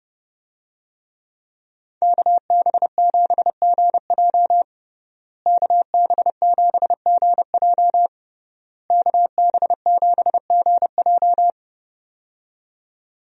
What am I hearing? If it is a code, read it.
Morse code, heard as a keyed tone, about 32 wpm, 700 Hz. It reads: KB7GJ KB7GJ KB7GJ
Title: USA Call Signs - 5 Characters - Repeated 3x - 30wpm